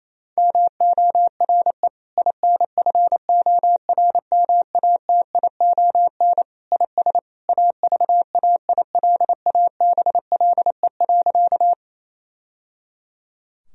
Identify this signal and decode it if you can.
Morse code, heard as a keyed tone, about 28 wpm, 700 Hz. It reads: MORE INFORMATION IS AVAILABLE.
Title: Sentences from Top 1000 Words - Review Part 3 - 28wpm